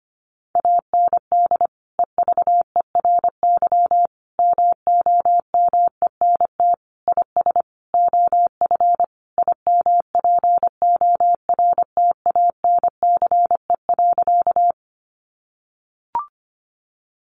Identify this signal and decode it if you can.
Morse code, heard as a keyed tone, about 25 wpm, 700 Hz. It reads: AND EVERY MOMENT IS OF IMPORTANCE.
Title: Sentences from Top 1200 Words 25wpm